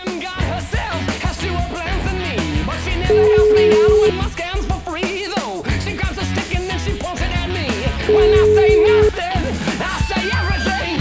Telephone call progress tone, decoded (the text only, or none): ringback tone